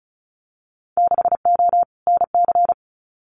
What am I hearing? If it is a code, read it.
Morse code, heard as a keyed tone, about 34 wpm, 700 Hz. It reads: 6O NC
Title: Winter Field Day Exchanges 35wpm